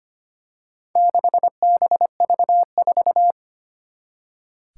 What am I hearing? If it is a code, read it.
Morse code, heard as a keyed tone, about 25 wpm, 700 Hz. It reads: 6BV4
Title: Four Challenging Letter-Number Combinations 25wpm